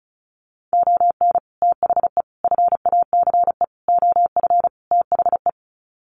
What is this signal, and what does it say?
Morse code, heard as a keyed tone, about 34 wpm, 700 Hz. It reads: ON THE FACE OF THE
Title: Sets of 5 Words Encore2 35wpm